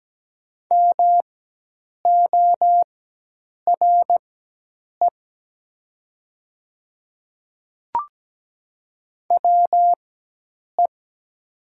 Morse code, heard as a keyed tone, about 17 wpm, spaced as for 8 wpm, 700 Hz. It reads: MORE WE
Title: 4x Character Spacing - Top 300 Words - 17wpm